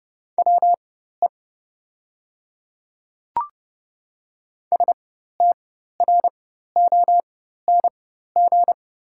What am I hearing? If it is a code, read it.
Morse code, heard as a keyed tone, about 30 wpm, spaced as for 14 wpm, 700 Hz. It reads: WE STRONG
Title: 4x Character Spacing - Top 500 Words - 30wpm